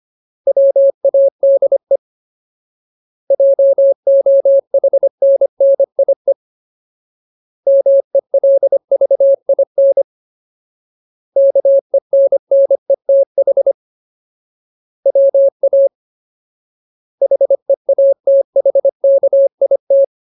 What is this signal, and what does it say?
Morse code, heard as a keyed tone, about 25 wpm, 550 Hz. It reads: WADE JOHNNIE MELVIN KENNETH WA HEATHKIT
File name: Ultimate Word List / 25WPM Ultimate